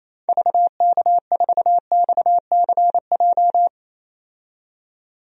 Morse code, heard as a keyed tone, about 28 wpm, 700 Hz. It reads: VK4XCJ
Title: DXCC-100 Call Signs - 6 Characters - 28wpm